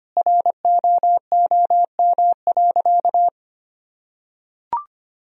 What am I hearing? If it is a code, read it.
Morse code, heard as a keyed tone, about 25 wpm, 690 Hz. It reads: ROOM.
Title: Sentences from Top 1000 Words - Review Part 4 - 25wpm